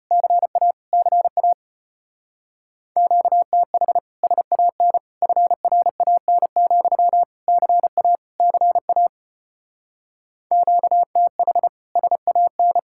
Morse code, heard as a keyed tone, about 31 wpm, 700 Hz. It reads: CA CA QTH SAN FRAN, CA CA QTH SAN
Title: Speed-Racing Common QSO Phrases Encore - 25wpm